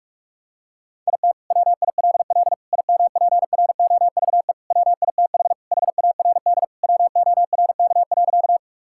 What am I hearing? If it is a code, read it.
Morse code, heard as a keyed tone, about 47 wpm, 700 Hz. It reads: IT WILL IMPROVE WITH HARD WORK.
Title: Sentences from Top 1000 Words - Review Part 5 - 45wpm